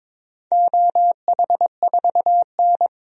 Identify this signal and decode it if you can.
Morse code, heard as a keyed tone, about 21 wpm, 700 Hz. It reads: OH4N
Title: DXCC-100 Call Signs - 4 Characters - 22wpm